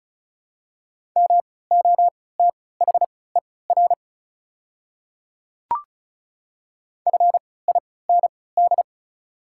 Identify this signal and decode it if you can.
Morse code, heard as a keyed tone, about 36 wpm, spaced as for 20 wpm, 700 Hz. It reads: MOTHER FIND
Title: 3x Character Spacing - Top 200 Words - 35wpm